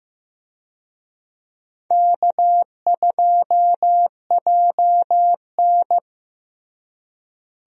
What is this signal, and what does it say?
Morse code, heard as a keyed tone, about 15 wpm, 700 Hz. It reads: K2JN
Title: USA Call Signs - 4 Characters - Repeated 2x - 15wpm